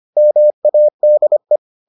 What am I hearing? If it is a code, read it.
Morse code, heard as a keyed tone, about 25 wpm, 600 Hz. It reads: MADE